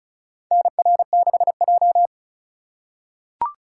Morse code, heard as a keyed tone, about 36 wpm, 700 Hz. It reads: NR6J
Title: USA Call Signs - 4 Characters - 2x Character Spacing - 35wpm